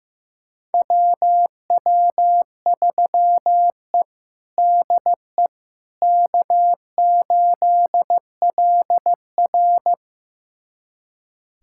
Morse code, heard as a keyed tone, about 15 wpm, 700 Hz. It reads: WW3E DE K8LR
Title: USA Callsign DE Callsign - 4 Characters - 15wpm